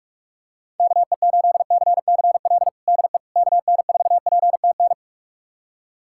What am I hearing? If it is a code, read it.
Morse code, heard as a keyed tone, about 46 wpm, 700 Hz. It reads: KE8CCL DE KN4PTN
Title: USA Callsign DE Callsign - 6 Characters - 45wpm